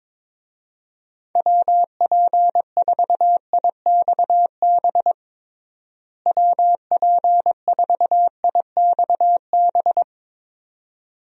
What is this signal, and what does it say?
Morse code, heard as a keyed tone, about 22 wpm, 700 Hz. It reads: WP4IXB WP4IXB
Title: USA Call Signs - 6 Characters - Repeated 2x - 22wpm